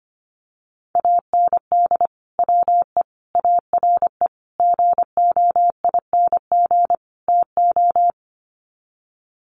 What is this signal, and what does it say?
Morse code, heard as a keyed tone, about 25 wpm, 700 Hz. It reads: AND WE ARE GOING TO
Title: Sets of 5 Words Encore2 25wpm